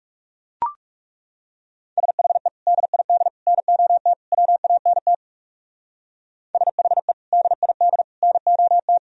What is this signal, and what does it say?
Morse code, heard as a keyed tone, about 38 wpm, 700 Hz. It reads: SHE DID NOT WANT SHE DID NOT